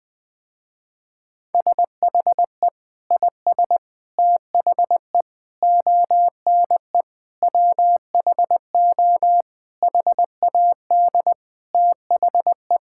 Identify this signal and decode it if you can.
Morse code, heard as a keyed tone, about 20 wpm, 700 Hz. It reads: SHE IS THE ONE WHO HAD THE
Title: Sentences with 2-Word Phrases - Spoken Beginning - 20wpm